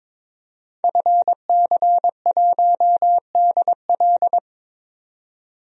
Morse code, heard as a keyed tone, about 22 wpm, 700 Hz. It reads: FC1DL
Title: DXCC-100 Call Signs - 5 Characters - 22wpm